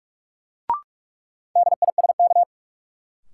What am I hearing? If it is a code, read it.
Morse code, heard as a keyed tone, about 44 wpm, 700 Hz. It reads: DISK
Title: Sentences from Top 1000 Words - Review Part 6 - New Word Drill - 45wpm